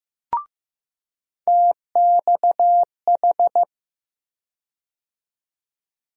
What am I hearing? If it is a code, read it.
Morse code, heard as a keyed tone, about 15 wpm, 700 Hz. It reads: TXH
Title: USA Call Sign Postfixes - 3 Letters 15wpm